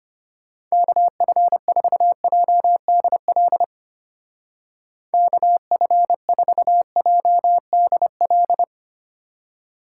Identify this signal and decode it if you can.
Morse code, heard as a keyed tone, about 32 wpm, 700 Hz. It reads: KF4JDL KF4JDL